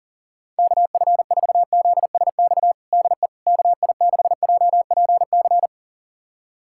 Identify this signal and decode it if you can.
Morse code, heard as a keyed tone, about 37 wpm, 700 Hz. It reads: KF4ZSX DE KI6JPC
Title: USA Callsign DE Callsign 40wpm